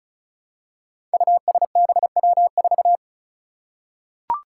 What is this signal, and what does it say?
Morse code, heard as a keyed tone, about 36 wpm, 700 Hz. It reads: USBW4
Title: Five Challenging Letter-Number Combinations 35wpm